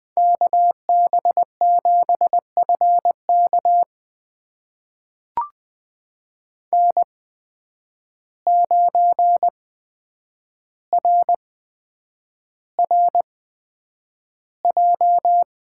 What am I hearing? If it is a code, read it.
Morse code, heard as a keyed tone, about 20 wpm, spaced as for 9 wpm, 700 Hz. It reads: KB7FK N9RRJ
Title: USA Call Signs - 5 Characters - 8x Character Spacing - 20wpm